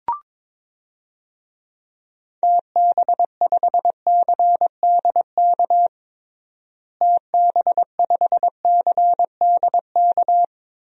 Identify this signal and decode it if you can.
Morse code, heard as a keyed tone, about 22 wpm, 700 Hz. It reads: TB5CDK TB5CDK
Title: DXCC-100 Call Signs - 6 Characters - Repeated 2x - 22wpm